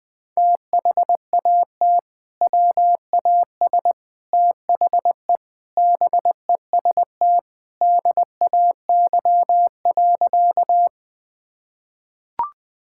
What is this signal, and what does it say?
Morse code, heard as a keyed tone, about 20 wpm, 700 Hz. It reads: THAT WAS THE BEST DAY.
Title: Sentences with 3-Word Phrases - 20wpm